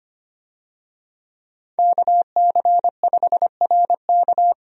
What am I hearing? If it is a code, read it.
Morse code, heard as a keyed tone, about 25 wpm, 700 Hz. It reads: KC5RK
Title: USA Call Signs - 5 Characters - 25wpm